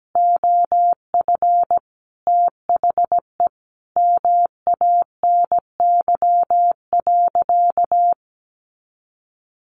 Morse code, heard as a keyed tone, about 17 wpm, 700 Hz. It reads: OF THE MANY.
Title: Sentences from Top 1300 Words 17wpm